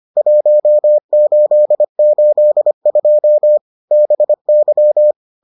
Morse code, heard as a keyed tone, about 25 wpm, 600 Hz. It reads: 1882 BY